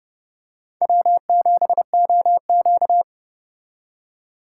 Morse code, heard as a keyed tone, about 30 wpm, 700 Hz. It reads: W7OQ
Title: K1USN SST QSO Practice - 30wpm